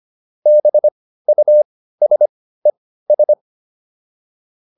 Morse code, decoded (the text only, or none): BUSES